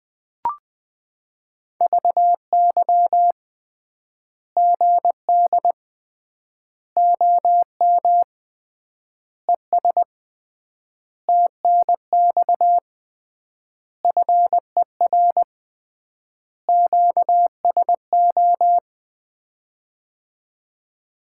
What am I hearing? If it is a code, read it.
Morse code, heard as a keyed tone, about 21 wpm, 700 Hz. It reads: VY GD OM ES TNX FER QSO
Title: Common QSO Phrases - 3x Word-Spacing - 20wpm